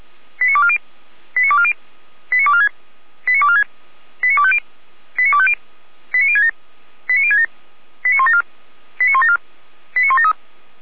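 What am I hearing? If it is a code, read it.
Five-tone selective call (ZVEI), heard as quick runs of five tones, tones of about 70 ms. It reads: 89240 89240 89246 89246 89250 89250 89067 89067 89174 89174 89173
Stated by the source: ZVEI five-tone selective call, code 89160, received by SDR, from Feuerwehr Orlamünde.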